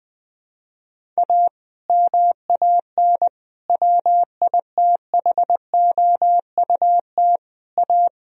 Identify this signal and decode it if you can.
Morse code, heard as a keyed tone, about 20 wpm, 700 Hz. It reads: A MAN WITHOUT A
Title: Sentences from Top 1000 Words - Review Part 3 - 20wpm